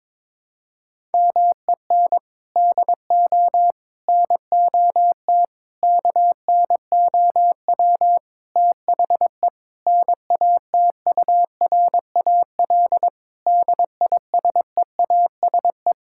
Morse code, heard as a keyed tone, about 22 wpm, 700 Hz. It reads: MEN DO NOT KNOW THE NATURAL DISEASE
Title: Sentences from Top 1000 Words - Review Part 6 - 22wpm